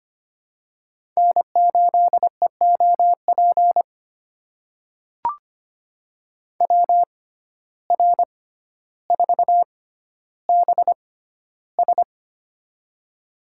Morse code, heard as a keyed tone, about 25 wpm, spaced as for 13 wpm, 700 Hz. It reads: N8EOP WR4BS
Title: USA Call Signs - 5 Characters - 6x Character Spacing - 25wpm